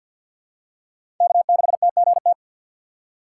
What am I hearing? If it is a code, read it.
Morse code, heard as a keyed tone, about 48 wpm, 700 Hz. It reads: K6TGT
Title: USA Call Signs - 6x Character Spacing - 50wpm